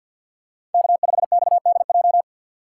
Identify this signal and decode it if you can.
Morse code, heard as a keyed tone, about 47 wpm, 700 Hz. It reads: K5XDJ
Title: USA Call Signs - 2x Character Spacing - 50wpm